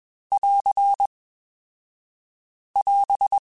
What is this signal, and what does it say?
Morse code, heard as a keyed tone, about 22 wpm, 2550 Hz. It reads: <AR> <AS>
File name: Alphabetical / Alphabetical mp3s / S vs H @20